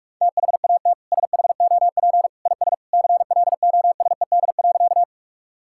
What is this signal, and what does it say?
Morse code, heard as a keyed tone, about 48 wpm, 700 Hz. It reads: THAT SHOP IS CLOSED.
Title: Sentences from Top 1600 Words 45wpm